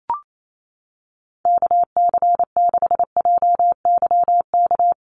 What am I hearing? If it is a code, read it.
Morse code, heard as a keyed tone, about 28 wpm, 700 Hz. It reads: KC6JYK
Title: USA Call Signs Repeated 3x 28wpm